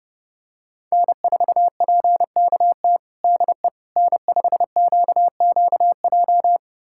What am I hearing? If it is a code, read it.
Morse code, heard as a keyed tone, about 30 wpm, 700 Hz. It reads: N4PKT DE N5QQJ
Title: USA Callsign DE Callsign - 5 Characters - 30wpm